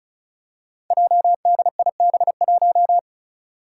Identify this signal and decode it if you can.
Morse code, heard as a keyed tone, about 36 wpm, 700 Hz. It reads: JDIB1
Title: Five Challenging Letter-Number Combinations 35wpm